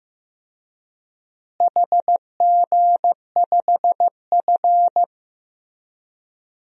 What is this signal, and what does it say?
Morse code, heard as a keyed tone, about 16 wpm, 700 Hz. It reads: HG5F
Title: DXCC-100 Call Signs - 4 Characters - Repeated 3x - 15wpm